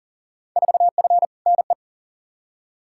Morse code, heard as a keyed tone, about 40 wpm, 700 Hz. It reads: 4F NE
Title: Field Day Exchanges Encore - 40wpm